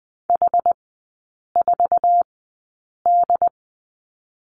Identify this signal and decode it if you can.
Morse code, heard as a keyed tone, about 20 wpm, spaced as for 8 wpm, 690 Hz. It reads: H4D
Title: 133 Lesson 22e - Question Mark - Three Characters 20wpm